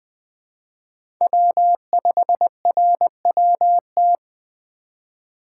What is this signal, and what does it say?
Morse code, heard as a keyed tone, about 21 wpm, 700 Hz. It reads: W5RWT
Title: USA Call Signs - 5 Characters - 2x Character Spacing - 20wpm